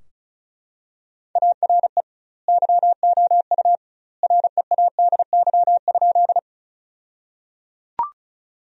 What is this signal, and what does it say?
Morse code, heard as a keyed tone, about 34 wpm, 700 Hz. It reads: ARE YOU READY?
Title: 2x Word-Spacing - Sentences from Top 400 Words - 35wpm